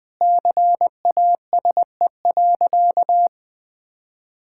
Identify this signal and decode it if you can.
Morse code, heard as a keyed tone, about 21 wpm, 700 Hz. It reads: CASE.
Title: Sentences with 2-Word Phrases - Spoken Beginning - 20wpm